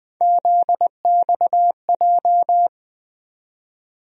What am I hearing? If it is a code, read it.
Morse code, heard as a keyed tone, about 20 wpm, 700 Hz. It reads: ZXJ
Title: USA Call Sign Postfixes - 3 Letters 20wpm